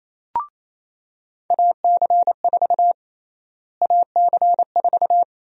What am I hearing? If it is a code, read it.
Morse code, heard as a keyed tone, about 27 wpm, 700 Hz. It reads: AC4 AC4
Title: USA Call Sign Prefixes - Repeated 2x - 28wpm